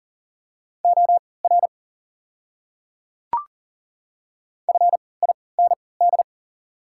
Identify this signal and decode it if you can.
Morse code, heard as a keyed tone, about 40 wpm, spaced as for 23 wpm, 690 Hz. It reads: OR FIND
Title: 3x Character Spacing - Top 500 Words - 40wpm